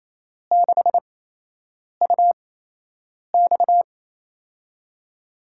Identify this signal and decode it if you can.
Morse code, heard as a keyed tone, about 28 wpm, spaced as for 8 wpm, 700 Hz. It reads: 6UX
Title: USA Call Signs - 8x Character Spacing - 28wpm